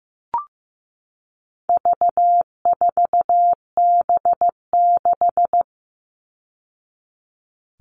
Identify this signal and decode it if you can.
Morse code, heard as a keyed tone, about 15 wpm, 700 Hz. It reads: V4B6
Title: All Cut Numbers - 4-Digit 15wpm